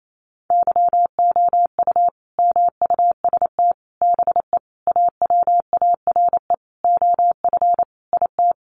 Morse code, heard as a keyed tone, about 28 wpm, 700 Hz. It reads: YOU MUST BE AWARE OF IT
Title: Sentences from Top 1200 Words 28wpm